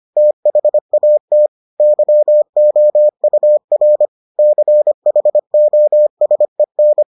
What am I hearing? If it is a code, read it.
Morse code, heard as a keyed tone, about 25 wpm, 600 Hz. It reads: THAT YOUR CHOSEN